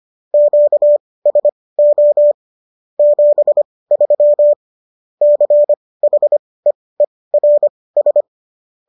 Morse code, heard as a keyed tone, about 25 wpm, spaced as for 18 wpm, 590 Hz. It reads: QSO 73 CHEERS